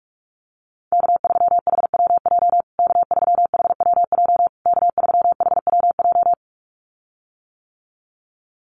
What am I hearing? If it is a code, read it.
Morse code, heard as a keyed tone, about 44 wpm, 700 Hz. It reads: K3HWJ K3HWJ K3HWJ
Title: USA Call Signs Repeated 3x 45wpm